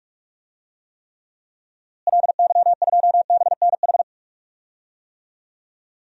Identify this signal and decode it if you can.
Morse code, heard as a keyed tone, about 46 wpm, 700 Hz. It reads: LY2BNH
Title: DXCC-100 Call Signs - 6 Characters - Repeated 3x - 45wpm